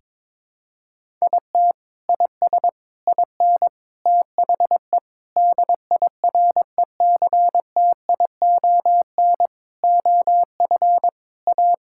Morse code, heard as a keyed tone, about 22 wpm, 700 Hz. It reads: IT IS IN THE DIRECTION OF A
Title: Sentences from Top 1000 Words - Review Part 5 - 22wpm